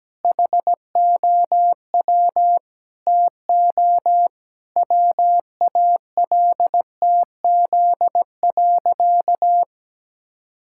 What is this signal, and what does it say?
Morse code, heard as a keyed tone, about 17 wpm, 700 Hz. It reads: HOW TO WALTZ.
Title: Pangram Challenge v2 - 17wpm